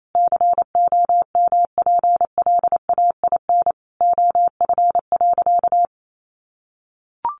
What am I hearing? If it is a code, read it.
Morse code, heard as a keyed tone, about 28 wpm, 700 Hz. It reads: COMPLAIN OF.
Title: Sentences from Top 1300 Words 28wpm